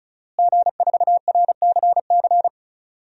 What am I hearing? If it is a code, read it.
Morse code, heard as a keyed tone, about 34 wpm, 700 Hz. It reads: G4RCC
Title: DXCC-100 Call Signs - 5 Characters - Repeated 2x - 35wpm